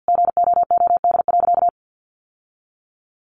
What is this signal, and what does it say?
Morse code, heard as a keyed tone, about 50 wpm, 700 Hz. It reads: GOOD.
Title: Sentences from Top 800 Words 50wpm